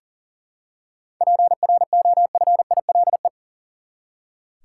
Morse code, heard as a keyed tone, about 40 wpm, 700 Hz. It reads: PROFILE
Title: Seven Letter Words - 40wpm